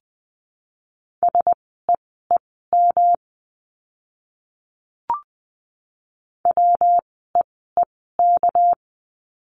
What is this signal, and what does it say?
Morse code, heard as a keyed tone, about 20 wpm, spaced as for 15 wpm, 700 Hz. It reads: SEEM WEEK
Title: 230 Lesson 37f - Letter x - Words 20wpm